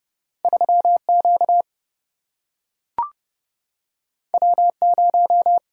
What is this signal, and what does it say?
Morse code, heard as a keyed tone, about 30 wpm, 700 Hz. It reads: 3Q W0
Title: Single-Letter Single-Number 30wpm